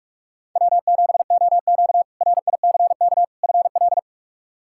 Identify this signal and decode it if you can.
Morse code, heard as a keyed tone, about 45 wpm, 700 Hz. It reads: W7OQ RICK FL
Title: K1USN SST QSO Practice - 45wpm